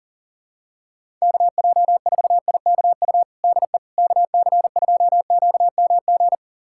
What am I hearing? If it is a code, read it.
Morse code, heard as a keyed tone, about 40 wpm, 700 Hz. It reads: KJ4IKU DE KC2QMG
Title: USA Callsign DE Callsign - 6 Characters - 40wpm